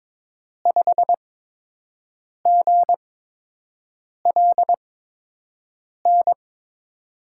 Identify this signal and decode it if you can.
Morse code, heard as a keyed tone, about 22 wpm, spaced as for 6 wpm, 700 Hz. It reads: 5GLN
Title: USA Call Signs - 8x Character Spacing - 22wpm